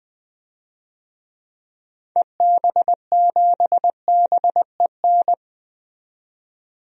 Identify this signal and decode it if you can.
Morse code, heard as a keyed tone, about 20 wpm, 700 Hz. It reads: EB7BEN
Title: DXCC-100 Call Signs - 6 Characters - 20wpm